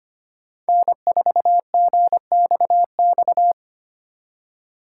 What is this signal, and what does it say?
Morse code, heard as a keyed tone, about 30 wpm, 700 Hz. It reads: N4GXX
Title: Speed Racing - USA 1x3 Call Signs 25wpm